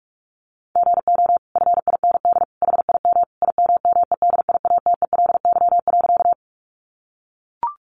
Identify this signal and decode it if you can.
Morse code, heard as a keyed tone, about 44 wpm, 700 Hz. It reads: GO FIND HIM IMMEDIATELY.
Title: Sentences from Top 1100 Words 45wpm